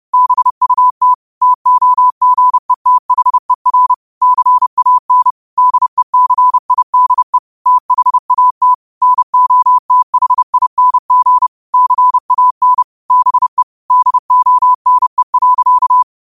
Morse code, heard as a keyed tone, about 30 wpm, 1000 Hz. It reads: BUT TOGETHER CAN DECIDE THAT NOTHING CAN BE DONE.
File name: qotdfiles / 30wpm